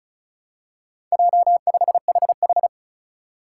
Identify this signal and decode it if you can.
Morse code, heard as a keyed tone, about 36 wpm, 700 Hz. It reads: J5HH
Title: Four Challenging Letter-Number Combinations 35wpm